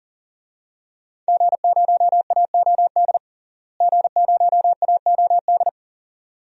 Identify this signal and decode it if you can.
Morse code, heard as a keyed tone, about 40 wpm, 700 Hz. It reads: G0AOD G0AOD
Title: DXCC-100 Call Signs - 5 Characters - Repeated 2x - 40wpm